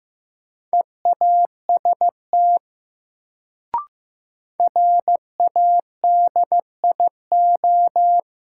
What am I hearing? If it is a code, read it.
Morse code, heard as a keyed tone, about 15 wpm, 700 Hz. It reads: EAST RADIO
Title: Sentences from Top 1000 Words - Review Part 3 - New Word Drill - 15wpm